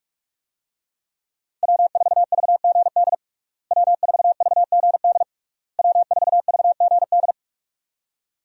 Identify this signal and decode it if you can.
Morse code, heard as a keyed tone, about 42 wpm, 700 Hz. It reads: W4VGD W4VGD W4VGD
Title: USA Call Signs - 5 Characters - Repeated 3x - 45wpm